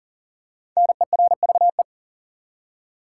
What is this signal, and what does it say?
Morse code, heard as a keyed tone, about 40 wpm, 700 Hz. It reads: NERVE
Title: Five Letter Words - 40wpm